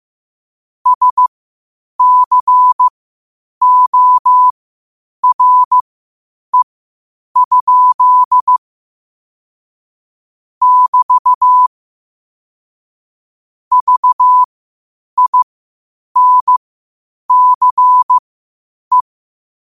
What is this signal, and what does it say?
Morse code, heard as a keyed tone, about 15 wpm, spaced as for 8 wpm, 1000 Hz. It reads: SCORE? <BT> VINCE
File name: qotdfiles / 8wpm